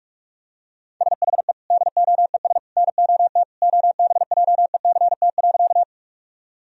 Morse code, heard as a keyed tone, about 42 wpm, 700 Hz. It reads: SHE DOES NOT OBJECT.
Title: Sentences from Top 1000 Words - Review Part 3 - 45wpm